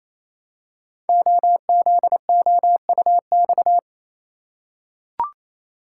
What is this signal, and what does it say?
Morse code, heard as a keyed tone, about 27 wpm, 700 Hz. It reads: OZOUX
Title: Code Groups - 5 Letter:Numbers - 28wpm